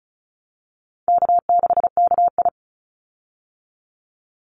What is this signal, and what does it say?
Morse code, heard as a keyed tone, about 34 wpm, 700 Hz. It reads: K6KI
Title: Easy Call Signs Repeated 3x - 35wpm